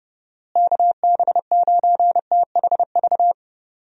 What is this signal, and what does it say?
Morse code, heard as a keyed tone, about 30 wpm, 700 Hz. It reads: KB9THV
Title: USA Call Signs - 6 Characters - Repeated 2x - 30wpm